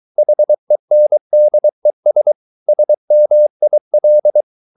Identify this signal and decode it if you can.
Morse code, heard as a keyed tone, about 23 wpm, 600 Hz. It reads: HENDES SMIL
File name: Ferie QRS 23wpm